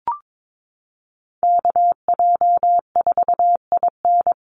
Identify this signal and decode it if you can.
Morse code, heard as a keyed tone, about 22 wpm, 700 Hz. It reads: KJ4IN
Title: USA Call Signs 22wpm